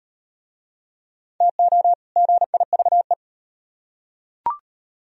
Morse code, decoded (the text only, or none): TO GIVE